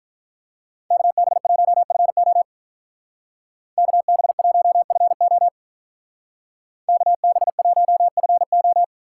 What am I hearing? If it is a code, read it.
Morse code, heard as a keyed tone, about 43 wpm, 700 Hz. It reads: KB1FO KB1FO KB1FO
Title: Speed Racing - USA 2x2 Call Signs 35wpm